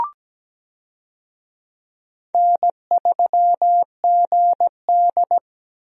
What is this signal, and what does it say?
Morse code, heard as a keyed tone, about 17 wpm, 700 Hz. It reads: N3GD
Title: USA Call Signs - 4 Characters - Repeated 3x - 17wpm